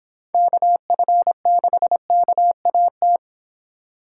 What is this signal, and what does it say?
Morse code, heard as a keyed tone, about 25 wpm, 700 Hz. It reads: KF6KAT